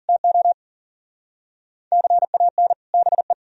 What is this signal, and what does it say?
Morse code, heard as a keyed tone, about 53 wpm, 700 Hz. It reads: TO CAN BE